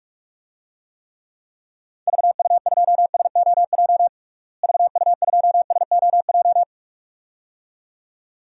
Morse code, heard as a keyed tone, about 42 wpm, 700 Hz. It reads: VU2SOJ VU2SOJ
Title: DXCC-100 Call Signs - 6 Characters - Repeated 2x - 45wpm